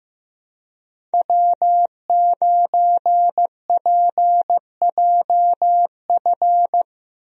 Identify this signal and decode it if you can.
Morse code, heard as a keyed tone, about 15 wpm, 700 Hz. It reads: W9PJF
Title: USA Call Signs - 5 Characters - 15wpm